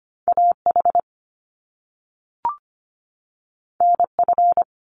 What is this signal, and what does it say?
Morse code, heard as a keyed tone, about 25 wpm, 700 Hz. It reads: A H N F
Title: Top Two Letter Combinations in Words 25wpm